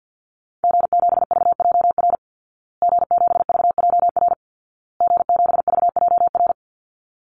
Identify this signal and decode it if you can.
Morse code, heard as a keyed tone, about 50 wpm, 700 Hz. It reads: G7VJR G7VJR G7VJR
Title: Call Signs Repeated 2x - 50wpm